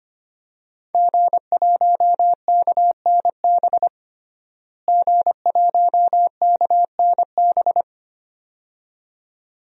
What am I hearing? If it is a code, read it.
Morse code, heard as a keyed tone, about 25 wpm, 700 Hz. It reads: G1KNB G1KNB
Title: DXCC-100 Call Signs - 5 Characters - Repeated 2x - 25wpm